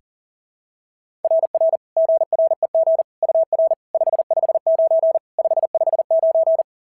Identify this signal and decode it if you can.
Morse code, heard as a keyed tone, about 37 wpm, 660 Hz. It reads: RR GREG UR 559 559
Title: SOTA QSO Practice 40wpm